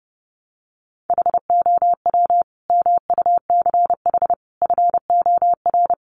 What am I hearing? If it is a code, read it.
Morse code, heard as a keyed tone, about 30 wpm, 700 Hz. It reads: HOW MUCH FOR
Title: Sentences from Top 1500 Words 30wpm